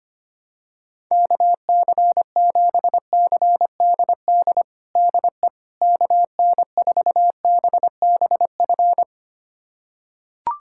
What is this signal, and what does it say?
Morse code, heard as a keyed tone, about 25 wpm, 700 Hz. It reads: KC7CDD DE KN4BBF
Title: USA Callsign DE Callsign - 6 Characters - 25wpm